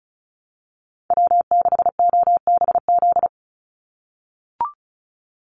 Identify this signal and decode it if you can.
Morse code, heard as a keyed tone, about 34 wpm, 700 Hz. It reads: W6OBZ
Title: USA Call Signs 35wpm